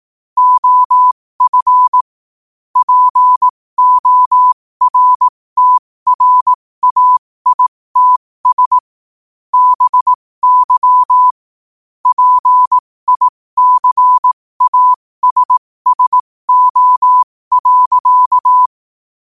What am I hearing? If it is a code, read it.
Morse code, heard as a keyed tone, about 18 wpm, spaced as for 15 wpm, 1000 Hz. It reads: OF PORTRAITS BY PICASSO.
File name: qotdfiles / 15wpm